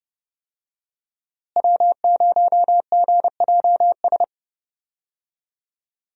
Morse code, heard as a keyed tone, about 30 wpm, 700 Hz. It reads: W0GJS
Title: USA Call Signs - 5 Characters - Repeated 3x - 30wpm